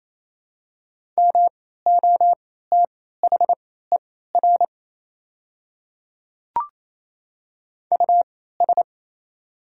Morse code, heard as a keyed tone, about 28 wpm, spaced as for 16 wpm, 700 Hz. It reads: MOTHER US
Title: 3x Character Spacing - Top 200 Words - 28wpm